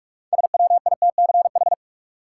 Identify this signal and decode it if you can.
Morse code, heard as a keyed tone, about 41 wpm, 700 Hz. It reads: SWITCH
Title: Six Letter Words - 45wpm